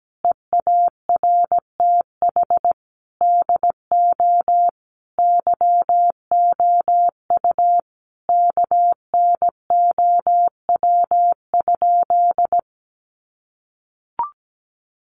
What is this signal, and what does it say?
Morse code, heard as a keyed tone, about 17 wpm, 700 Hz. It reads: EARTH DO YOU KNOW?
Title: Sentences from Top 200 Words 17wpm